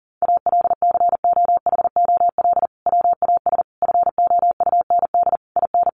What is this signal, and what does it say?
Morse code, heard as a keyed tone, about 40 wpm, 700 Hz. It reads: ALCOHOL WAS FOUND IN
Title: Sentences from Top 1400 Words 40wpm